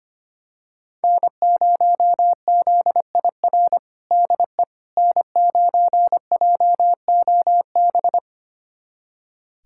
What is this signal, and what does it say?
Morse code, heard as a keyed tone, about 25 wpm, 700 Hz. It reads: N0ZIR DE N9JOB
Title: USA Callsign DE Callsign - 5 Characters - 25wpm